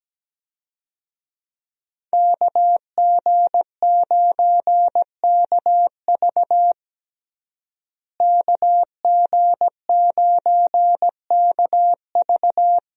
Morse code, heard as a keyed tone, about 17 wpm, 700 Hz. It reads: KG9KV KG9KV
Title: USA Call Signs - 5 Characters - Repeated 2x - 17wpm